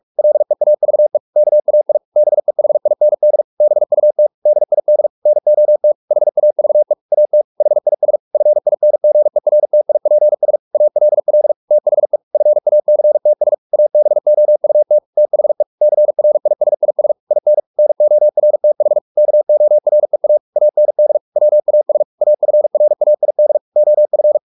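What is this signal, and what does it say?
Morse code, heard as a keyed tone, about 44 wpm, 600 Hz. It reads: LEAVE KAI BEHIND BUT DID NOT HAVE AT HIS FINGERTIPS ALL THE FACTS ABOUT THE CRISIS IN NORTH KOREA AND WAS AFRAID OF
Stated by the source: Morse practice at 45 wpm, the novel Never by Ken Follett in English, from CW med Gnister.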